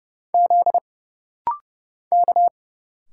Morse code, heard as a keyed tone, about 30 wpm, 700 Hz. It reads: Z K
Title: Single Letters - Rapid-Fire 30wpm